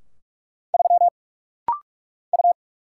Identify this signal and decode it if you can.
Morse code, heard as a keyed tone, about 44 wpm, 700 Hz. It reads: 3 U
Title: Single Letter-Number Alt - Rapid-Fire 45wpm